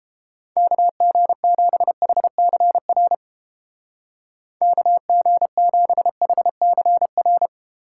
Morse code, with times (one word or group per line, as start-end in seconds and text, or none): 0.6-3.1 KG7HCR
4.6-7.5 KG7HCR